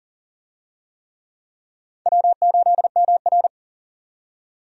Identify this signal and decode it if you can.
Morse code, heard as a keyed tone, about 40 wpm, 700 Hz. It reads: W8MR
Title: USA Call Signs - 4 Characters - 40wpm